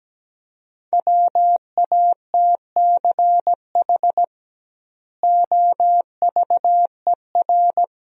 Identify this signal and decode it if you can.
Morse code, heard as a keyed tone, about 17 wpm, 700 Hz. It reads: WATCH OVER
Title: 2x Word-Spacing - Sentences from Top 300 Words - 17wpm